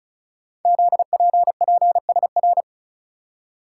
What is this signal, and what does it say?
Morse code, heard as a keyed tone, about 36 wpm, 690 Hz. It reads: ZPPSR
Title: Code Groups - 5 Letters - 35wpm